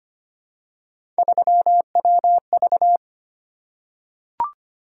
Morse code, heard as a keyed tone, about 25 wpm, 700 Hz. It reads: 3WV
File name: Three Letter-Number Combinations - 25wpm